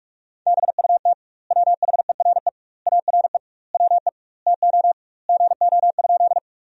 Morse code, heard as a keyed tone, about 46 wpm, 700 Hz. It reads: BUT WHERE ARE WE TO GO?
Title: 2x Word-Spacing - Sentences from Top 200 Words - 45wpm